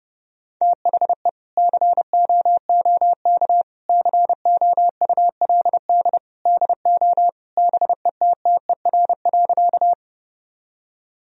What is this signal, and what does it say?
Morse code, heard as a keyed tone, about 30 wpm, 700 Hz. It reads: THE COOK COULD DO BETTER.
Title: Sentences from Top 1000 Words - Review Part 4 - 30wpm